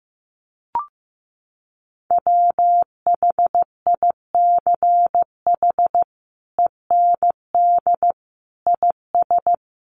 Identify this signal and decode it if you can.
Morse code, heard as a keyed tone, about 15 wpm, 700 Hz. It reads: WHICH END IS
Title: Sentences from Top 300 Words 15wpm